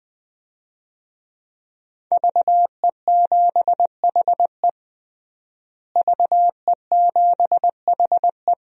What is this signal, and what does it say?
Morse code, heard as a keyed tone, about 21 wpm, 700 Hz. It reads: VE7HE VE7HE
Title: DXCC-100 Call Signs - 5 Characters - Repeated 2x - 20wpm